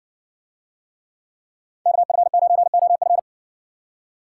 Morse code, heard as a keyed tone, about 56 wpm, 700 Hz. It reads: KF9OF